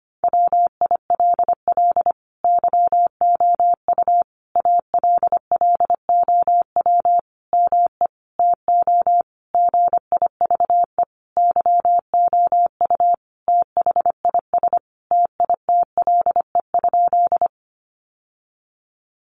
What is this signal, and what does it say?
Morse code, heard as a keyed tone, about 25 wpm, 700 Hz. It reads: WILL YOU ALLOW ME TO GIVE YOU THIS TITLE?
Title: Sentences from Top 1200 Words 25wpm